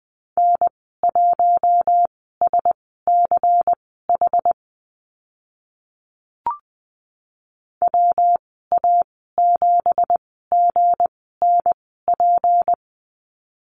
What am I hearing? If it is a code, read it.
Morse code, heard as a keyed tone, about 20 wpm, spaced as for 14 wpm, 700 Hz. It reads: N1SCH WA7GNP
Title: 179 Lesson 29g - Slash - Call Signs 20wpm